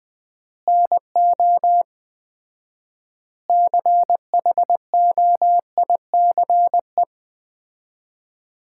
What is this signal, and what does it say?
Morse code, heard as a keyed tone, about 21 wpm, 700 Hz. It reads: NO CHOICE
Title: 4x Word-Spacing - Sets of 5 Words - 20wpm